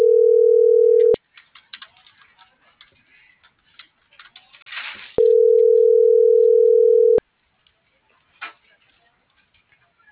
Ringback tone (North America), 440 + 480 Hz.